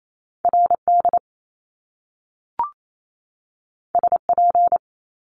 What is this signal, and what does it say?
Morse code, heard as a keyed tone, about 28 wpm, 700 Hz. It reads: RD SP
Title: Top Two Letter Combinations in Words 28wpm